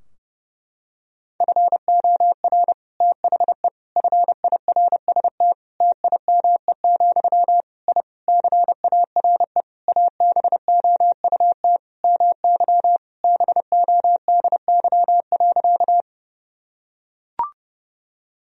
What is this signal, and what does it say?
Morse code, heard as a keyed tone, about 30 wpm, 700 Hz. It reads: FOR THE FIRST TIME, I CARE ABOUT MY BODY.
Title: Sentences from Top 1000 Words - Review Part 2 - 30wpm